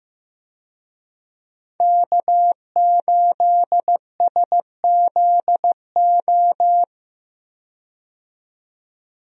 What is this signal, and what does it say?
Morse code, heard as a keyed tone, about 15 wpm, 700 Hz. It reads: K8SZO
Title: USA Call Signs - 5 Characters - 15wpm